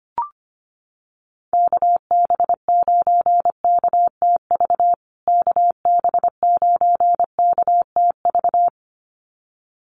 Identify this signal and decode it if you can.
Morse code, heard as a keyed tone, about 25 wpm, 700 Hz. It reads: KB9KTV KB9KTV
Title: USA Call Signs Repeated 2x 25wpm